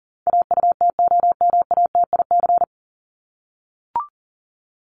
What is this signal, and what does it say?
Morse code, heard as a keyed tone, about 40 wpm, 700 Hz. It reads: AUTOMATIC
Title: Top 1751-2000 Words 40wpm